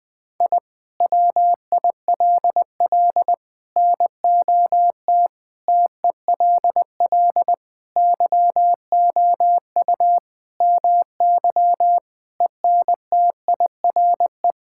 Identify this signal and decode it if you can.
Morse code, heard as a keyed tone, about 21 wpm, 700 Hz. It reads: I WILL NOT TELL YOU MY ENTIRE
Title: Sentences from Top 1000 Words - Review Part 5 - 20wpm